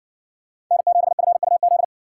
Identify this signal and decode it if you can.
Morse code, heard as a keyed tone, about 54 wpm, 700 Hz. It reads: N7FUZ